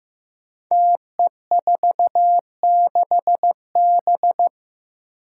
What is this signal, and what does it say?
Morse code, heard as a keyed tone, about 15 wpm, 700 Hz. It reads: TE46B
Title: Five Challenging Letter-Number Combinations 15wpm